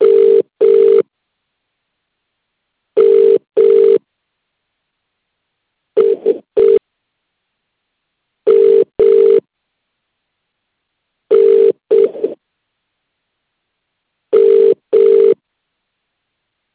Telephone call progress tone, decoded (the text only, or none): ringback tone